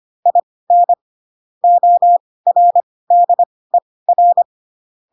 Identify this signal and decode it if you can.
Morse code, heard as a keyed tone, about 25 wpm, spaced as for 18 wpm, 700 Hz. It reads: IN ORDER